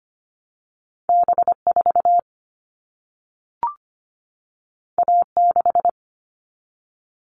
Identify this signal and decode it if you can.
Morse code, heard as a keyed tone, about 25 wpm, 700 Hz. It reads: B4 A6
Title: All Cut Numbers - 2-Digit 25wpm